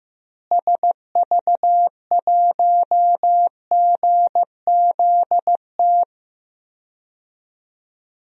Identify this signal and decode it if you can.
Morse code, heard as a keyed tone, about 15 wpm, 700 Hz. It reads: SV1GZT
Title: DXCC-100 Call Signs - 6 Characters - Repeated 3x - 15wpm